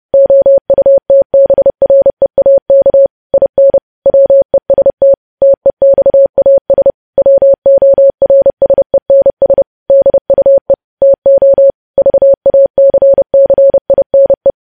Morse code, heard as a keyed tone, about 30 wpm, 550 Hz. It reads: OUTBREAK IN WEST TEXAS WORSENS DUE TO VACCINE
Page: news headlines in morse code at 30 wpm